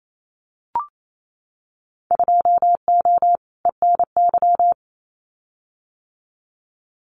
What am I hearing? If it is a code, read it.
Morse code, heard as a keyed tone, about 28 wpm, 700 Hz. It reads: 2O ENY
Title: Winter Field Day Exchanges 28wpm